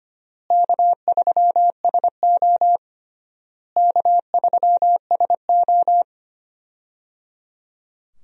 Morse code, heard as a keyed tone, about 25 wpm, 700 Hz. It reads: K3SO K3SO
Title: USA Call Signs - 4 Characters - Repeated 3x - 25wpm